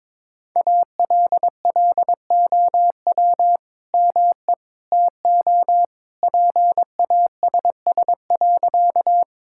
Morse code, heard as a keyed tone, about 22 wpm, 700 Hz. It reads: ALLOW ME TO PASS.
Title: Sentences from Top 1000 Words - Review Part 5 - 22wpm